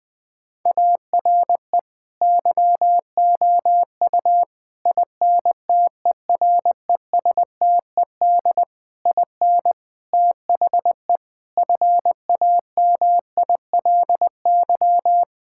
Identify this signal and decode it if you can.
Morse code, heard as a keyed tone, about 20 wpm, 700 Hz. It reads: ARE YOU INTERESTED IN THE FAMILY
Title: Sentences from Top 1000 Words - Review Part 6 - 20wpm